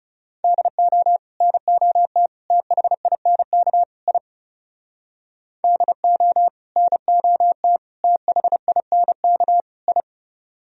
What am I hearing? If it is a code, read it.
Morse code, heard as a keyed tone, about 40 wpm, 700 Hz. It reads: DO NOT THINK I DO NOT THINK I